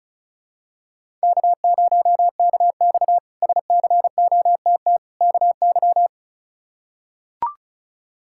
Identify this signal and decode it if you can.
Morse code, heard as a keyed tone, about 36 wpm, 700 Hz. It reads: K0KX SCOTT KY
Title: K1USN SST QSO Practice - 35wpm